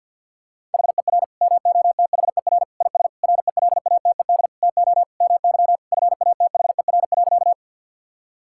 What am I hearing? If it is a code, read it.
Morse code, heard as a keyed tone, about 52 wpm, 690 Hz. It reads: HER MOTHER IS RELATED TO MY FATHER.
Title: Sentences from Top 1000 Words - Review Part 5 - 50wpm